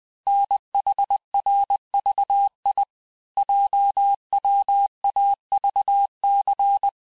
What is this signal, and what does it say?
Morse code, heard as a keyed tone, about 20 wpm, 800 Hz. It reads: NHRVI JWAVC
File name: cours-du-lundi files / cours-1584-teste-chiffre-vitesse-19mpm